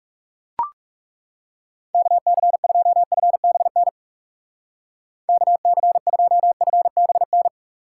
Morse code, heard as a keyed tone, about 39 wpm, 700 Hz. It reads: KC2FBN KC2FBN